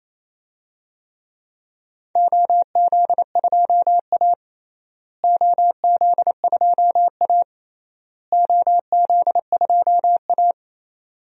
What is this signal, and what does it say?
Morse code, heard as a keyed tone, about 28 wpm, 700 Hz. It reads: OZ2A OZ2A OZ2A
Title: DXCC-100 Call Signs - 4 Characters - Repeated 3x - 28wpm